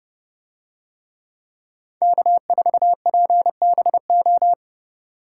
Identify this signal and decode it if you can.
Morse code, heard as a keyed tone, about 29 wpm, 700 Hz. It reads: K4PBO